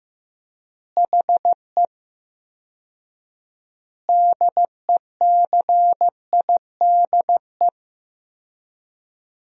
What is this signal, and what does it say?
Morse code, heard as a keyed tone, about 15 wpm, 700 Hz. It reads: HE DECIDE
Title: 4x Word-Spacing - Sentences from Top 500 Words - 15wpm